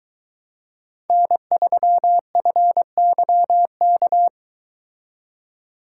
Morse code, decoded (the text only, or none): N3FYK